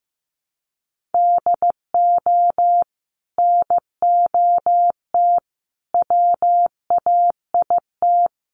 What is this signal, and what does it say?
Morse code, heard as a keyed tone, about 15 wpm, 700 Hz. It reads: DO NOT WAIT
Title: Sentences from Top 600 Words 15wpm